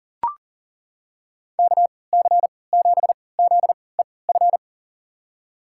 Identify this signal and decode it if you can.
Morse code, heard as a keyed tone, about 40 wpm, spaced as for 23 wpm, 700 Hz. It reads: KC7ZEF
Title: USA Call Signs - 6 Characters - 3x Character Spacing - 40wpm